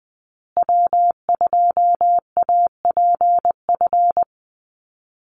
Morse code, heard as a keyed tone, about 19 wpm, 700 Hz. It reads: W2APF
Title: Call Signs Repeated 3x - 20wpm